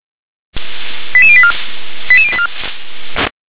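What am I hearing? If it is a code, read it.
Five-tone selective call (ZVEI), heard as quick runs of five tones, tones of about 70 ms. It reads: 88084 88084
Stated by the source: ZVEI five-tone selective call 88080, received via SDR.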